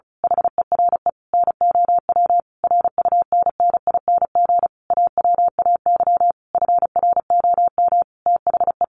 Morse code, heard as a keyed tone, about 34 wpm, 700 Hz. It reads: HERE NOW RUNNING AWAY FROM THE
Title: A Sound Of Thunder 35wpm